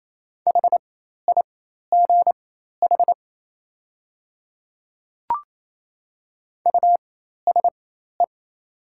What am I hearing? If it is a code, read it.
Morse code, heard as a keyed tone, about 28 wpm, spaced as for 13 wpm, 700 Hz. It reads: HIGH USE
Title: 4x Character Spacing - Top 300 Words - 28wpm